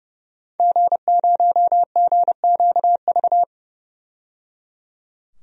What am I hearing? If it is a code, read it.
Morse code, heard as a keyed tone, about 29 wpm, 700 Hz. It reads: G0GQV
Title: DXCC-100 Call Signs - 5 Characters - 30wpm